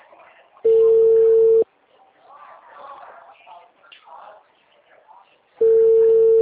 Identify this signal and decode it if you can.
Ringback tone (China, 450 Hz), with speech.
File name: out- -2004